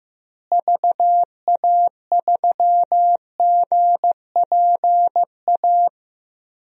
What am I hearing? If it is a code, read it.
Morse code, heard as a keyed tone, about 15 wpm, 700 Hz. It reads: VA3GPA
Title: DXCC-100 Call Signs - 6 Characters - Repeated 2x - 15wpm